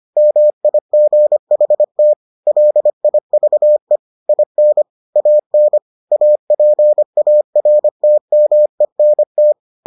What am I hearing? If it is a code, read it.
Morse code, heard as a keyed tone, about 25 wpm, 600 Hz. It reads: MIGHT LIVE IN AN APARTMENT